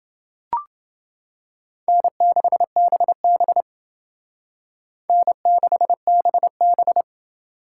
Morse code, heard as a keyed tone, about 28 wpm, 700 Hz. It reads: N6BB N6BB